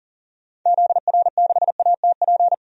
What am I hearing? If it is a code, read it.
Morse code, heard as a keyed tone, about 40 wpm, 700 Hz. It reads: ZR6ATP
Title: DXCC-100 Call Signs - 6 Characters - 40wpm